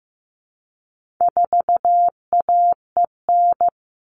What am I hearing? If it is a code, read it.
Morse code, heard as a keyed tone, about 15 wpm, 700 Hz. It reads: 4AEN
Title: All Cut Numbers - 4-Digit 15wpm